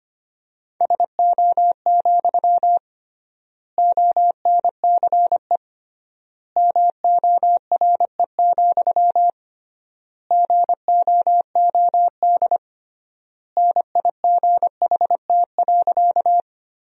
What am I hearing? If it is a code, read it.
Morse code, heard as a keyed tone, about 25 wpm, 700 Hz. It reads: SO, ONCE MORE, GOOD NIGHT.
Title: 3x Word-Spacing - Sentences from Top 300 Words - 25wpm